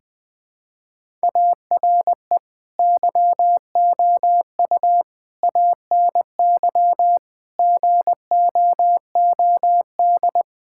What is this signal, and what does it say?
Morse code, heard as a keyed tone, about 19 wpm, 700 Hz. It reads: ARE YOU ANY GOOD
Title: Sentences from Top 1000 Words - Review Part 6 - 20wpm